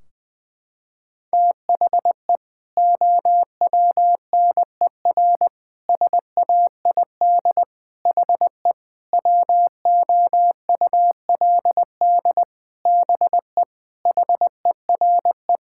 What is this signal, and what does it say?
Morse code, heard as a keyed tone, about 20 wpm, 700 Hz. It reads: THE OWNER SAID HE WOULD BE HERE
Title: Sentences from Top 1600 Words 20wpm